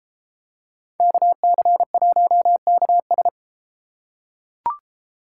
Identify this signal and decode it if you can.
Morse code, heard as a keyed tone, about 33 wpm, 700 Hz. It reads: KC1KS